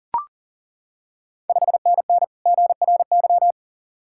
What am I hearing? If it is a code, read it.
Morse code, heard as a keyed tone, about 37 wpm, 700 Hz. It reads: 5NN GRY
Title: Ontario QSO Party Exchange 40wpm